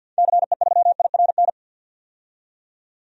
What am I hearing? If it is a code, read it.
Morse code, heard as a keyed tone, about 46 wpm, 700 Hz. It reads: CE3IRN
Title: DXCC-100 Call Signs - 6 Characters - 50wpm